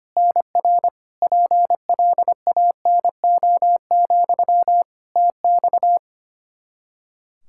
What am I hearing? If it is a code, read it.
Morse code, heard as a keyed tone, about 25 wpm, 700 Hz. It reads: NR PLANO, TX
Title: Practice QSO - Exchange 1 - Beginning - 25wpm